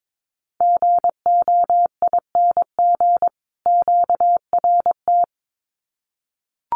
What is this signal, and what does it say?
Morse code, heard as a keyed tone, about 22 wpm, 700 Hz. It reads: GOING QRT
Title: Common QSO Phrases 22wpm